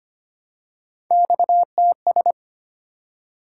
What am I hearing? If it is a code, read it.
Morse code, heard as a keyed tone, about 25 wpm, 700 Hz. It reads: XTS
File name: Three Letter-Number Combinations - 25wpm